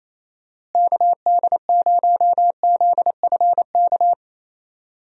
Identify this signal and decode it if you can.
Morse code, heard as a keyed tone, about 28 wpm, 700 Hz. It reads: KD0ZFK
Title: USA Call Signs - 6 Characters - Repeated 3x - 28wpm